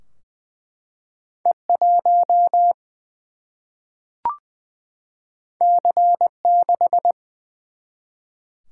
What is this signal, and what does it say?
Morse code, heard as a keyed tone, about 20 wpm, 700 Hz. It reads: E 1 C 6
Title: Single-Letter Single-Number 20wpm